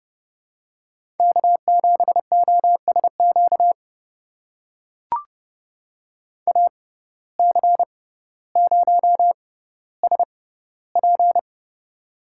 Morse code, heard as a keyed tone, about 30 wpm, spaced as for 15 wpm, 700 Hz. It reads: K7OSQ AC0SP
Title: USA Call Signs - 5 Characters - 6x Character Spacing - 30wpm